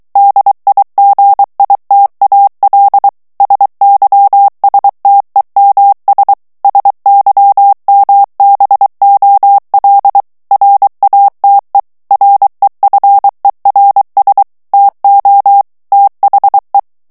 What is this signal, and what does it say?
Morse code, heard as a keyed tone, about 23 wpm, 800 Hz. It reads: DIGITAL SYSTEMS SYMBOL RATE REFERS TO THE
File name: ANGLAIS texte clair 3 vitesse 22 mots minute